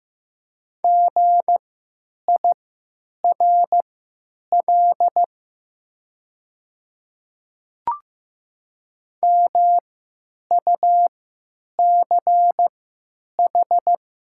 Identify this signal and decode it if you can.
Morse code, heard as a keyed tone, about 15 wpm, spaced as for 9 wpm, 700 Hz. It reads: GIRL MUCH
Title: 3x Character Spacing - Top 500 Words - 15wpm